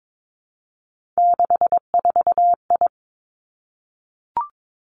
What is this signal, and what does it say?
Morse code, heard as a keyed tone, about 22 wpm, 700 Hz. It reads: 64I
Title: Three Challenging Letter-Number Combinations 22wpm